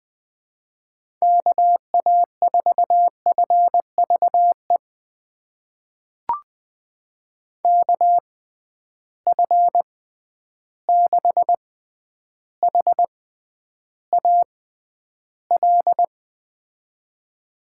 Morse code, heard as a keyed tone, about 20 wpm, spaced as for 10 wpm, 700 Hz. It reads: KA4FVE KF6HAL
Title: USA Call Signs - 6 Characters - 6x Character Spacing - 20wpm